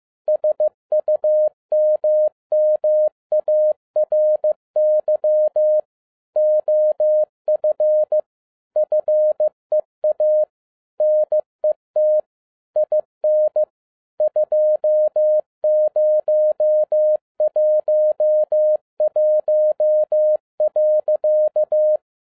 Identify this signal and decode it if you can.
Morse code, heard as a keyed tone, about 15 wpm, 610 Hz. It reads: SUMMARY OF FEA NET IN 2011.
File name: 2012-01-mp3 / n17-SUMMARY OF FEA NET IN 2011-15wpm